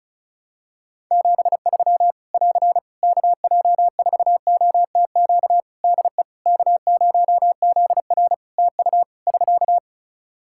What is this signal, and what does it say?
Morse code, heard as a keyed tone, about 34 wpm, 700 Hz. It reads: 73 <AR> KJ4OTQ DE K0ZR TU <SK>
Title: Practice QSO - Exchange 4 - Ending - 35wpm